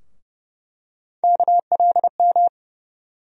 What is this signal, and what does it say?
Morse code, heard as a keyed tone, about 30 wpm, 700 Hz. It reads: KLM
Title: USA Call Sign Postfixes - 3 Letters 30wpm